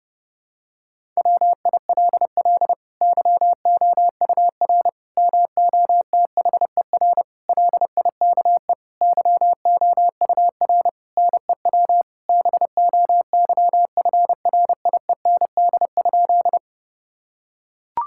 Morse code, heard as a keyed tone, about 32 wpm, 700 Hz. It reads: WILL YOUR MOTHER LIKE YOUR NEW BOYFRIEND?
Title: Sentences from Top 1000 Words - Review Part 5 - 30wpm